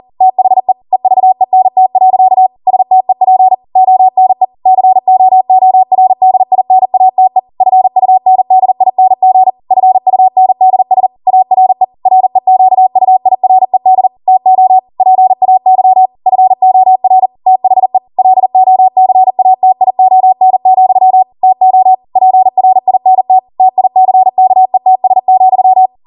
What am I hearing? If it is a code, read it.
Morse code, heard as a keyed tone, about 40 wpm, 750 Hz. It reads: THE EVENT. STEP ONE COORDINATE FUNDING FUNDS ARE REQUIRED TO PAY FOR THE LOCATION, TO PRINT TICKETS,